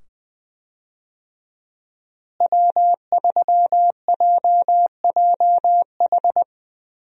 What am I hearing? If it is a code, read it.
Morse code, heard as a keyed tone, about 21 wpm, 700 Hz. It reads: W3JJH
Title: USA Call Signs - 5 Characters - Repeated 2x - 20wpm